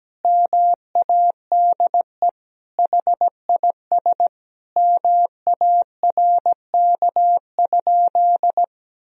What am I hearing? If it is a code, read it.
Morse code, heard as a keyed tone, about 17 wpm, 700 Hz. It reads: MADE HIS MARK?
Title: Sentences from Top 300 Words and Abbreviations 17wpm